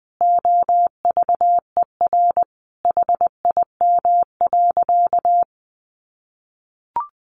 Morse code, heard as a keyed tone, about 20 wpm, 700 Hz. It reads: OVER HIM.
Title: Sentences from Top 1100 Words 20wpm